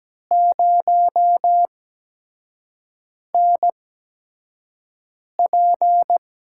Morse code, heard as a keyed tone, about 17 wpm, spaced as for 5 wpm, 700 Hz. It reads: 0NP
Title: USA Call Signs - 6 Characters - 8x Character Spacing - 17wpm